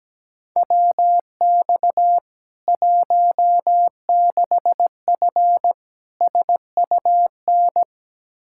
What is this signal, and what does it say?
Morse code, heard as a keyed tone, about 17 wpm, 700 Hz. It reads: WX 16F SUN
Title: K1USN SST QSO Practice 17wpm